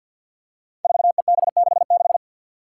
Morse code, heard as a keyed tone, about 53 wpm, 700 Hz. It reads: 4EB66
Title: Five Challenging Letter-Number Combinations 50wpm